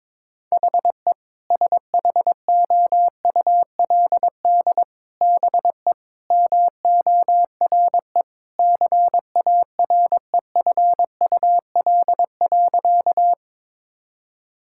Morse code, heard as a keyed tone, about 22 wpm, 700 Hz. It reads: HE SHOULD BE MORE CAREFUL.
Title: Sentences from Top 1600 Words 22wpm